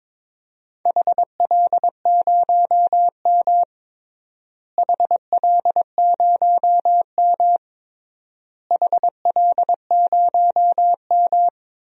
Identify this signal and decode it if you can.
Morse code, heard as a keyed tone, about 22 wpm, 700 Hz. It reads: HL0M HL0M HL0M
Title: DXCC-100 Call Signs - 4 Characters - Repeated 3x - 22wpm